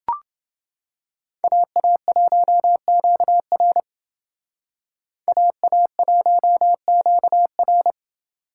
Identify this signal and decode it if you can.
Morse code, heard as a keyed tone, about 28 wpm, 700 Hz. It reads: AA1QR AA1QR